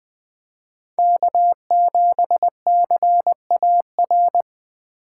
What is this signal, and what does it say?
Morse code, heard as a keyed tone, about 20 wpm, 700 Hz. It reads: K7CAR
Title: 2023 POTA Activator Call Signs - 20wpm